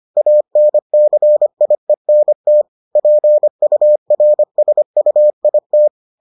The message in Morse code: ANCIENT PURSUIT

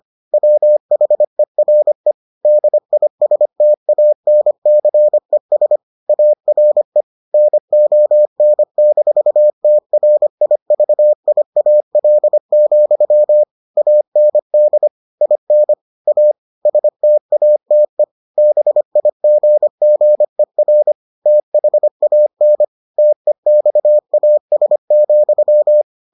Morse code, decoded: WHERE DISTANCES ARE NON-TRIVIAL, AND IN A STATE BIGGER THAN TEXAS,

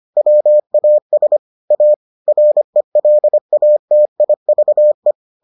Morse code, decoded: WAS A RELATIVE

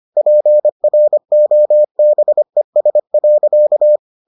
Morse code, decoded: PROBES.

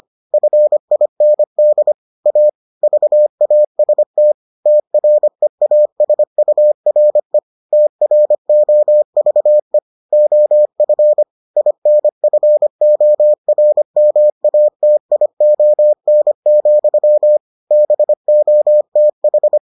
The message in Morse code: FIND A VAST TREASURE TROVE OF INFORMATION, BOTH

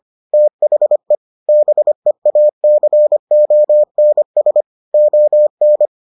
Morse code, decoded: THE BEACONS ON

